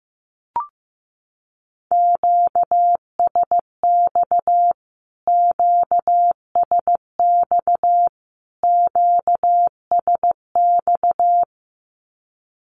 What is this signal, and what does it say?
Morse code, heard as a keyed tone, about 15 wpm, 700 Hz. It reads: QSX QSX QSX
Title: Q-Codes Repeated 3x 15wpm